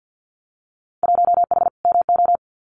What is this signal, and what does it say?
Morse code, heard as a keyed tone, about 50 wpm, 700 Hz. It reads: 1H MO
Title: Winter Field Day Exchanges 50wpm